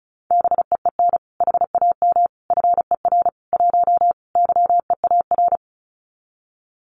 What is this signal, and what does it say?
Morse code, heard as a keyed tone, about 36 wpm, 700 Hz. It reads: BEEN HAM FER 1 YEAR
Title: Common QSO Phrases Encore 35wpm